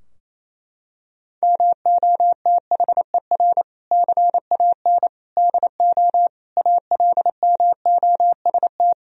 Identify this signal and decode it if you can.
Morse code, heard as a keyed tone, about 28 wpm, 700 Hz. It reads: MOTHER CAN DO ALMOST
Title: Sentences from Top 1000 Words - Review Part 4 - 28wpm